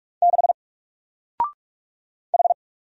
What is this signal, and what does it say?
Morse code, heard as a keyed tone, about 44 wpm, 700 Hz. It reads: 6 H
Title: Single Character (Letter:Number:Punctuation:Prosign) 45wpm